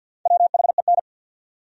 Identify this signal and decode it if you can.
Morse code, heard as a keyed tone, about 53 wpm, 700 Hz. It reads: WHEN
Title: Top 100 Words 50wpm